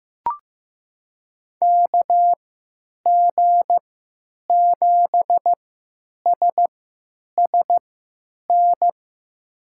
Morse code, heard as a keyed tone, about 15 wpm, spaced as for 9 wpm, 700 Hz. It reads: KG7SSN
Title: USA Call Signs - 6 Characters - 3x Character Spacing - 15wpm